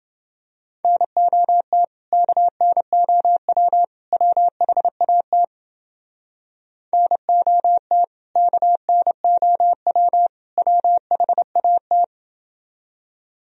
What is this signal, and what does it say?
Morse code, heard as a keyed tone, about 29 wpm, 700 Hz. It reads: NOT KNOW WHAT NOT KNOW WHAT